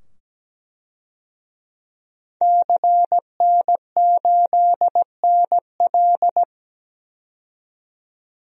Morse code, heard as a keyed tone, about 17 wpm, 700 Hz. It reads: CN8NL
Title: DXCC-100 Call Signs - 5 Characters - 17wpm